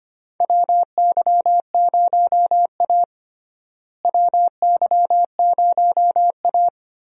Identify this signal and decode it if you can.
Morse code, heard as a keyed tone, about 25 wpm, 700 Hz. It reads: WY0A WY0A
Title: USA Call Signs - 4 Characters - Repeated 2x - 25wpm